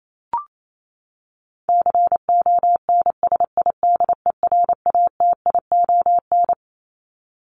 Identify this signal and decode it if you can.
Morse code, heard as a keyed tone, about 28 wpm, 700 Hz. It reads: CONSIDERATION
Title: Top 1251-1500 Words 28wpm